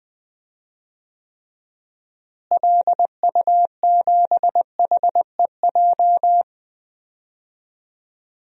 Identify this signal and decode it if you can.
Morse code, heard as a keyed tone, about 20 wpm, 700 Hz. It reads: LU7HEJ
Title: DXCC-100 Call Signs - 6 Characters - Repeated 3x - 20wpm